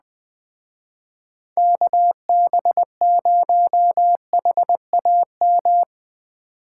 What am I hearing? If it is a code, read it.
Morse code, heard as a keyed tone, about 20 wpm, 700 Hz. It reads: KB0HAM
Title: Speed Racing - USA 2x3 Call Signs 17wpm